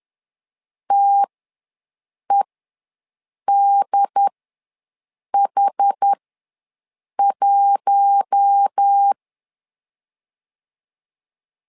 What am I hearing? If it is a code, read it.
Morse code, heard as a keyed tone, about 10 wpm, spaced as for 6 wpm, 800 Hz. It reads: TEDH1
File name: LEZIONI / 4 Q1S3 ws 10wpm 2437